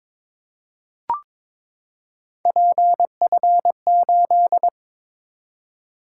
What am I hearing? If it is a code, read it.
Morse code, heard as a keyed tone, about 21 wpm, 700 Hz. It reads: PF8
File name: practice / 22wpm / Three Letter-Number Combinations - 22wpm